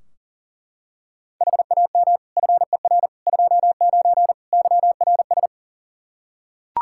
Morse code, heard as a keyed tone, about 40 wpm, 700 Hz. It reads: HAM FER 29 YRS
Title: K1USN SST QSO Practice 40wpm